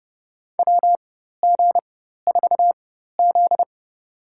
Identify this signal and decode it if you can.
Morse code, heard as a keyed tone, about 32 wpm, spaced as for 14 wpm, 700 Hz. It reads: WG4Z
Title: USA Call Signs - 4x Character Spacing - 30wpm